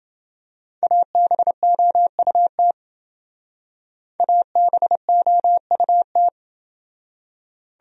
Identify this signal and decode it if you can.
Morse code, heard as a keyed tone, about 27 wpm, 700 Hz. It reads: ABOUT ABOUT